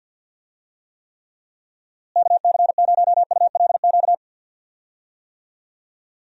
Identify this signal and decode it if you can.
Morse code, heard as a keyed tone, about 52 wpm, 700 Hz. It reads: KC0ULQ
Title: USA Call Signs - 6 Characters - Repeated 3x - 50wpm